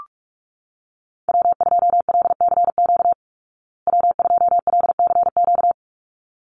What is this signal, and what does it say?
Morse code, heard as a keyed tone, about 44 wpm, 700 Hz. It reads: W2LCQ W2LCQ
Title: Call Signs Repeated 3x - 45wpm